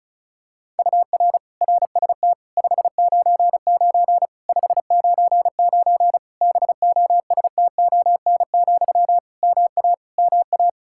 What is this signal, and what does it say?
Morse code, heard as a keyed tone, about 36 wpm, 700 Hz. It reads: UR RST 599 599 BOSTON, MA MA
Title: Practice QSO - Exchange 1 - Beginning - 35wpm